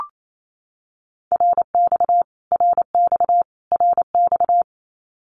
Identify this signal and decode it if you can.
Morse code, heard as a keyed tone, about 28 wpm, 700 Hz. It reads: RX RX RX
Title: Abbreviations Repeated 3x - 28wpm